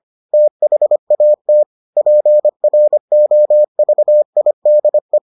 Morse code, heard as a keyed tone, about 25 wpm, 600 Hz. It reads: THAT PROVIDE